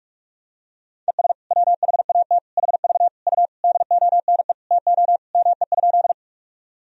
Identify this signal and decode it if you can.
Morse code, heard as a keyed tone, about 46 wpm, 700 Hz. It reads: ES WHAT HV U DONE TO ME?
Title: Sentences from Top 500 Words and Abbreviations 45wpm